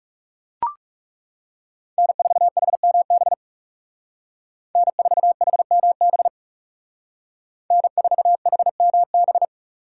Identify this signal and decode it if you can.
Morse code, heard as a keyed tone, about 36 wpm, 700 Hz. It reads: N4HMB N4HMB N4HMB
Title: Speed Racing - USA 1x3 Call Signs 30wpm